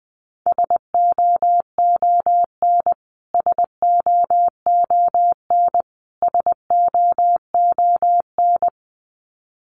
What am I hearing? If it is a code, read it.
Morse code, heard as a keyed tone, about 20 wpm, 700 Hz. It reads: SOON SOON SOON
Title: Top 500 Words Repeated 20wpm